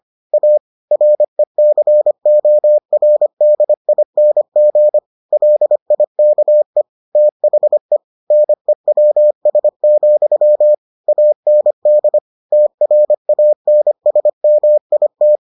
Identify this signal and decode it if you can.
Morse code, heard as a keyed tone, about 25 wpm, 600 Hz. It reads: A RECORDING LIKE THE NEWS, AND TRANSMIT